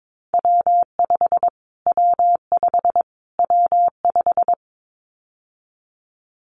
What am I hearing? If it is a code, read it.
Morse code, heard as a keyed tone, about 22 wpm, 700 Hz. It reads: W5 W5 W5
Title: Callsign Prefixes for Top 25 DXCC - Repeated 3x - 22wpm